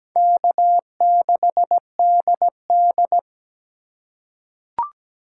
Morse code, heard as a keyed tone, about 17 wpm, 700 Hz. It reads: K6DD
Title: USA Call Signs - 4 Characters - 4x Character Spacing - 17wpm